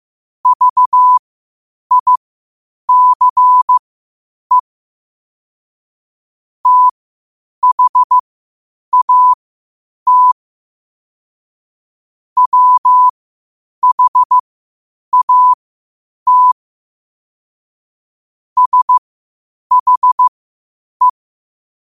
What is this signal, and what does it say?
Morse code, heard as a keyed tone, about 15 wpm, spaced as for 8 wpm, 1000 Hz. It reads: VICE THAT WHAT SHE
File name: qotdfiles / 8wpm